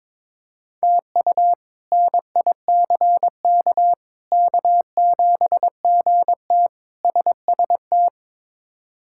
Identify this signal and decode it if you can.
Morse code, heard as a keyed tone, about 22 wpm, 700 Hz. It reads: TU NICK K7GT SST
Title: K1USN SST QSO Practice - 22wpm